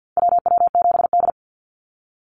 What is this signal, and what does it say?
Morse code, heard as a keyed tone, about 50 wpm, 700 Hz. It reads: WW7D
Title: NA SOTA Activator Callsigns 50wpm